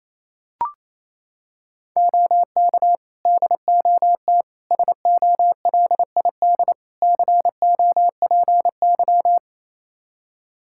Morse code, heard as a keyed tone, about 27 wpm, 700 Hz. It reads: OK DOT SOLID COPY
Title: Practice QSO - Exchange 3 - More Info - 28wpm